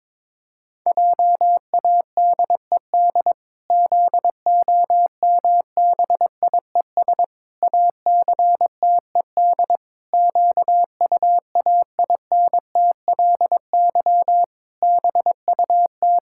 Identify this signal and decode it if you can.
Morse code, heard as a keyed tone, about 22 wpm, 700 Hz. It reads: JADED ZOMBIES ACTED QUAINTLY BUT
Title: Pangram Challenge v2 - 22wpm